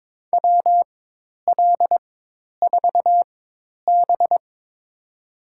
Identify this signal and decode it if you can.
Morse code, heard as a keyed tone, about 22 wpm, spaced as for 10 wpm, 700 Hz. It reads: WL4B
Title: USA Call Signs - 4 Characters - 4x Character Spacing - 22wpm